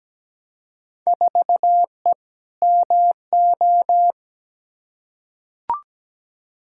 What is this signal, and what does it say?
Morse code, heard as a keyed tone, about 17 wpm, 700 Hz. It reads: 4E MO
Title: Field Day Exchanges Encore - 17wpm